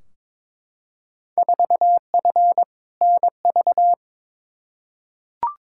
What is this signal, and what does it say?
Morse code, heard as a keyed tone, about 21 wpm, 700 Hz. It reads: 4F NV
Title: Field Day Exchanges Encore - 22wpm